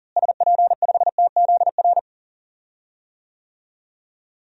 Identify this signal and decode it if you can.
Morse code, heard as a keyed tone, about 40 wpm, 700 Hz. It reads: SP5TZR
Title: DXCC-100 Call Signs - 6 Characters - Repeated 3x - 40wpm